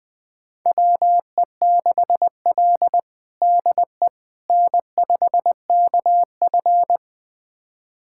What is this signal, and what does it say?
Morse code, heard as a keyed tone, about 20 wpm, 700 Hz. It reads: WE6L DE N5KF
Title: USA Callsign DE Callsign - 4 Characters - 20wpm